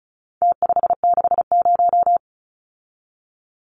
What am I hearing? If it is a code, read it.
Morse code, heard as a keyed tone, about 34 wpm, 700 Hz. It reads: T560
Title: Numbers - Contest Style - 4-Digit 35wpm